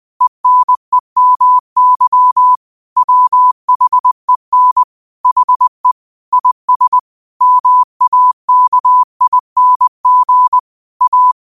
Morse code, heard as a keyed tone, about 20 wpm, 1000 Hz. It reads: ENEMY WHEN HE IS MAKING A
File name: qotdfiles / 20wpm